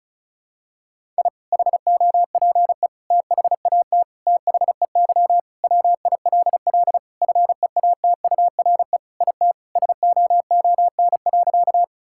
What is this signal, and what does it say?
Morse code, heard as a keyed tone, about 34 wpm, 700 Hz. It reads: I HOPE THAT THEY WILL FEATURE IT SOON.
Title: Sentences from Top 1000 Words - Review Part 5 - 35wpm